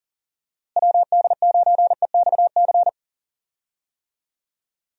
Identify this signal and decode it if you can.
Morse code, heard as a keyed tone, about 40 wpm, 700 Hz. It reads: WD9EXC
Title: USA Call Signs - 6 Characters - Repeated 3x - 40wpm